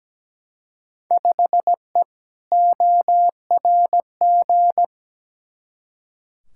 Morse code, heard as a keyed tone, about 17 wpm, 700 Hz. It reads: 5E ORG
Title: Field Day Exchanges Encore - 17wpm